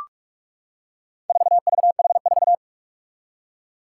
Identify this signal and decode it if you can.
Morse code, heard as a keyed tone, about 44 wpm, 700 Hz. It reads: 4VH4
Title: Four Challenging Letter-Number Combinations 45wpm